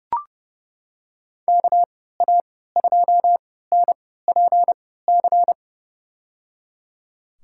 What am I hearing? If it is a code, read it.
Morse code, heard as a keyed tone, about 30 wpm, spaced as for 17 wpm, 700 Hz. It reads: KA2NPC
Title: USA Call Signs - 6 Characters - 3x Character Spacing - 30wpm